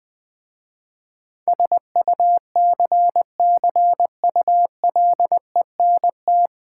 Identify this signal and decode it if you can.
Morse code, heard as a keyed tone, about 20 wpm, 700 Hz. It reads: SUCCULENT
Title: Difficult Words with Hints - 20wpm